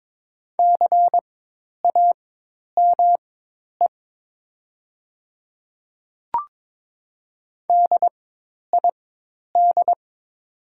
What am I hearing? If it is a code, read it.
Morse code, heard as a keyed tone, about 22 wpm, spaced as for 10 wpm, 700 Hz. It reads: CAME DID
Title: 4x Character Spacing - Top 200 Words - 22wpm